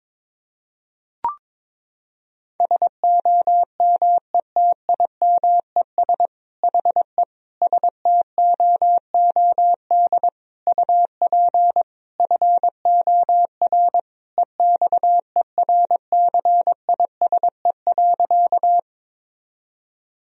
Morse code, heard as a keyed tone, about 22 wpm, 700 Hz. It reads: SOMETIMES HE STOOD UP FOR EXERCISE.
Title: Sentences from Top 1000 Words - Review Part 4 - 22wpm